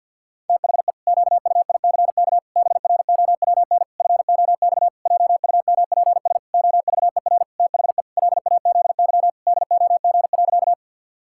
Morse code, heard as a keyed tone, about 50 wpm, 700 Hz. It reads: THE QUICK BROWN FOX JUMPS OVER THE LAZY DOG.
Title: Pangram Challenge v2 - 50wpm